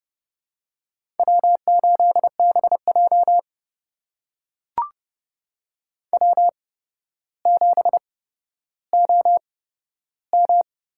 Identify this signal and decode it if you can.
Morse code, heard as a keyed tone, about 30 wpm, spaced as for 13 wpm, 700 Hz. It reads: W8BJ W7OM
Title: USA Call Signs - 4 Characters - 8x Character Spacing - 30wpm